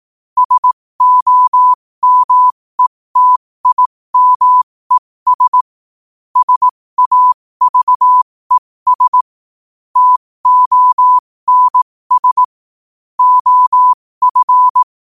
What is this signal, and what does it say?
Morse code, heard as a keyed tone, about 18 wpm, spaced as for 15 wpm, 1000 Hz. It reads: SOMETIMES SAVES TONS OF
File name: qotdfiles / 15wpm